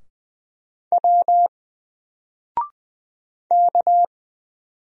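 Morse code, heard as a keyed tone, about 20 wpm, 700 Hz. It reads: W K
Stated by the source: Single letter and number Morse drill at 20 wpm, from N4LCD.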